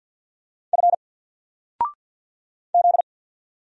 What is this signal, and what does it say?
Morse code, heard as a keyed tone, about 50 wpm, 700 Hz. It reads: F Z
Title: Single Letter-Number 50wpm